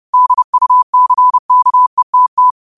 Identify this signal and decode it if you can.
Morse code, heard as a keyed tone, about 30 wpm, 1000 Hz. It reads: DUCKETT
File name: qotdfiles / 30wpm